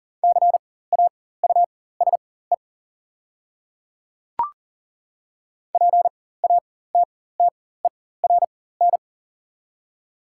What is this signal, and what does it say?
Morse code, heard as a keyed tone, about 40 wpm, spaced as for 19 wpm, 700 Hz. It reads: CAUSE PATTERN
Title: 4x Character Spacing - Top 500 Words - 40wpm